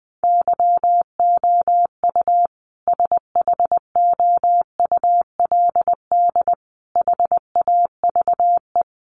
Morse code, heard as a keyed tone, about 20 wpm, 700 Hz. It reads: YOU SHOULD HAVE
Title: Sentences from Top 1200 Words 20wpm